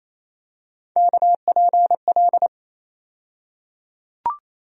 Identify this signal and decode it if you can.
Morse code, heard as a keyed tone, about 28 wpm, 700 Hz. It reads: KPL
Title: USA Call Sign Postfixes - 3 Letters 28wpm